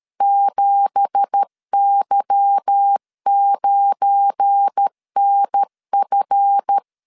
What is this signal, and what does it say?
Morse code, heard as a keyed tone, about 13 wpm, 800 Hz. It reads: 7Y9NF